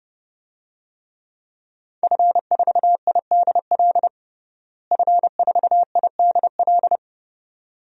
Morse code, heard as a keyed tone, about 32 wpm, 700 Hz. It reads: F4IDL F4IDL
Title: DXCC-100 Call Signs - 5 Characters - Repeated 2x - 30wpm